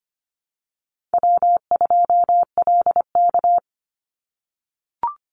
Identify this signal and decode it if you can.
Morse code, heard as a keyed tone, about 25 wpm, 700 Hz. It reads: W2LK
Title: Easy Call Signs - 25wpm